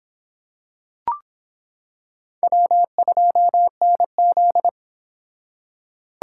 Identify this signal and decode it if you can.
Morse code, heard as a keyed tone, about 26 wpm, 700 Hz. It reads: W2NZ